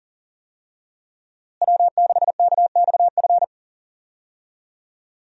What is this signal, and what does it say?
Morse code, heard as a keyed tone, about 40 wpm, 700 Hz. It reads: W6KXF
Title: USA Call Signs - 5 Characters - Repeated 2x - 40wpm